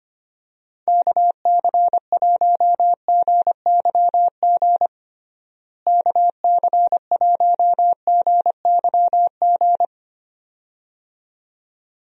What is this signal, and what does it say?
Morse code, heard as a keyed tone, about 25 wpm, 700 Hz. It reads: KC1GYG KC1GYG
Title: USA Call Signs - 6 Characters - Repeated 2x - 25wpm